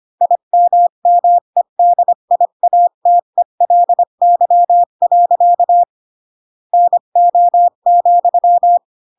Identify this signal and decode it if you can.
Morse code, heard as a keyed tone, about 25 wpm, 700 Hz. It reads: IMMEDIATELY. NO,
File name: Pinocchio / Pinocchio ch1 20 25 ik1hge